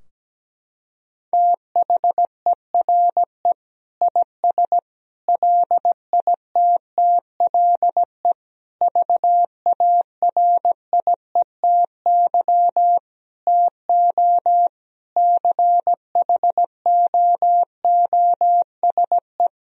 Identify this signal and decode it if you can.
Morse code, heard as a keyed tone, about 17 wpm, 700 Hz. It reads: THERE IS LITTLE VARIETY TO CHOOSE
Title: Sentences from Top 1000 Words - Review Part 4 - 17wpm